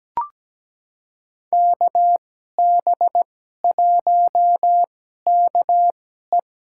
Morse code, heard as a keyed tone, about 17 wpm, spaced as for 12 wpm, 700 Hz. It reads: KB1KE
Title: USA Call Signs - 5 Characters - 2x Character Spacing - 17wpm